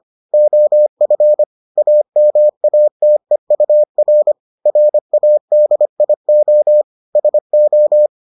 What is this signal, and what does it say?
Morse code, heard as a keyed tone, about 25 wpm, 600 Hz. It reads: OF AMATEUR RADIO SO